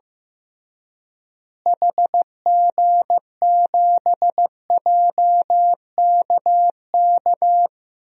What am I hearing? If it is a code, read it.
Morse code, heard as a keyed tone, about 15 wpm, 700 Hz. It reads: HG7JKK
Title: DXCC-100 Call Signs - 6 Characters - Repeated 3x - 15wpm